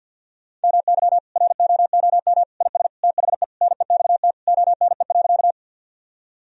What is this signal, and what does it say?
Morse code, heard as a keyed tone, about 46 wpm, 700 Hz. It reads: MY ROOM IS THE NEXT ONE.
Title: Sentences from Top 1000 Words - Review Part 1 - 50wpm